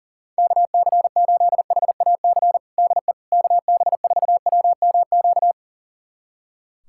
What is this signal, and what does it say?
Morse code, heard as a keyed tone, about 37 wpm, 700 Hz. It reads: KC8HAC DE KB4WMQ
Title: USA Callsign DE Callsign - 6 Characters - 40wpm